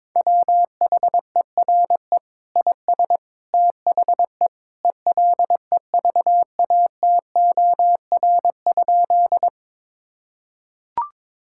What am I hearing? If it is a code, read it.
Morse code, heard as a keyed tone, about 21 wpm, 700 Hz. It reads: WHERE IS THE ELEVATOR?
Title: Sentences from Top 1600 Words 22wpm